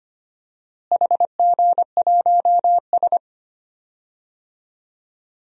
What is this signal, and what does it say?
Morse code, heard as a keyed tone, about 25 wpm, 700 Hz. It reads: HG1S
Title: DXCC-100 Call Signs - 4 Characters - Repeated 2x - 25wpm